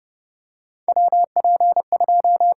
Morse code, heard as a keyed tone, about 30 wpm, 700 Hz. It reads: WP2
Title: USA Call Signs - 6 Characters - 8x Character Spacing - 30wpm